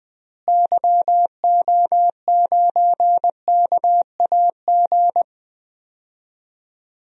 Morse code, heard as a keyed tone, about 21 wpm, 700 Hz. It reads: YO9KAG
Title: DXCC-100 Call Signs - 6 Characters - Repeated 3x - 20wpm